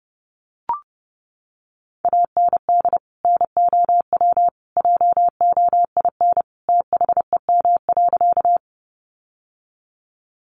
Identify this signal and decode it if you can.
Morse code, heard as a keyed tone, about 30 wpm, 700 Hz. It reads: AND NOW JOIN THEM.
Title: Sentences from Top 1400 Words 30wpm